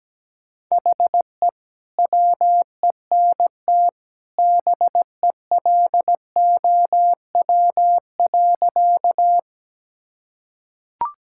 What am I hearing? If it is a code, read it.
Morse code, heard as a keyed tone, about 17 wpm, 700 Hz. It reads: HE WENT BELOW.
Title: Sentences from Top 1000 Words - Review Part 5 - 17wpm